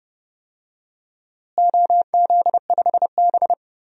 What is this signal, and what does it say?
Morse code, heard as a keyed tone, about 30 wpm, 700 Hz. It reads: OZ5B
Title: DXCC-100 Call Signs - 4 Characters - Repeated 3x - 30wpm